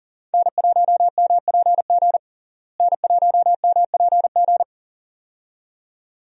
Morse code, heard as a keyed tone, about 37 wpm, 700 Hz. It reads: N1MPG N1MPG
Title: USA Call Signs - 5 Characters - Repeated 2x - 40wpm